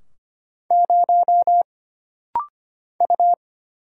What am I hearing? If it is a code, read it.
Morse code, heard as a keyed tone, about 25 wpm, 700 Hz. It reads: 0 U
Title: Single Letter-Number - Rapid-Fire 25wpm